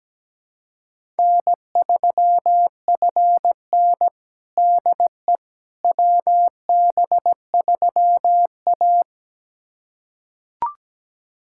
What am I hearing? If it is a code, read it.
Morse code, heard as a keyed tone, about 17 wpm, 700 Hz. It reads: N3FN DE WB3A
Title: USA Callsign DE Callsign - 4 Characters - 17wpm